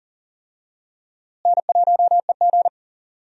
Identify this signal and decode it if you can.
Morse code, heard as a keyed tone, about 39 wpm, 700 Hz. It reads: N1EG
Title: USA Call Signs - 4 Characters - Repeated 3x - 40wpm